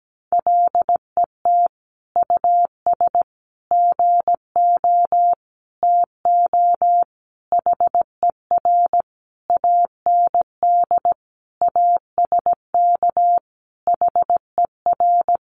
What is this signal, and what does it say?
Morse code, heard as a keyed tone, about 17 wpm, 700 Hz. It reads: LET US GO TO HER AND ASK HER
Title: Sentences from Top 1100 Words 17wpm